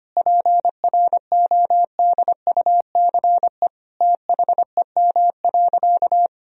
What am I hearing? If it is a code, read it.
Morse code, heard as a keyed tone, about 25 wpm, 700 Hz. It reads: PRODUCE THEM.
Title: Sentences from Top 1000 Words - Review Part 3 - 25wpm